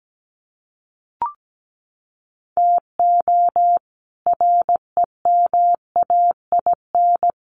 Morse code, heard as a keyed tone, about 17 wpm, 700 Hz. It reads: TO REMAIN
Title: Sentences from Top 1500 Words 17wpm